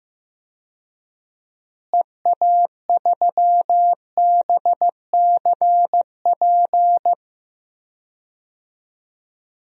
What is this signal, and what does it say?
Morse code, heard as a keyed tone, about 15 wpm, 700 Hz. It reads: EA3BCP
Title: DXCC-100 Call Signs - 6 Characters - Repeated 3x - 15wpm